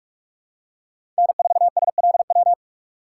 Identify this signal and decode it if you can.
Morse code, heard as a keyed tone, about 41 wpm, 700 Hz. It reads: N4SLW
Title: USA Call Signs - 6x Character Spacing - 45wpm